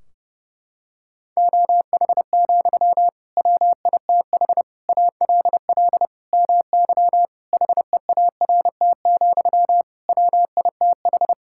Morse code, heard as a keyed tone, about 30 wpm, 700 Hz. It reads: OH, WITH ALL MY HEART, WITH
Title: Sentences from Top 1600 Words 30wpm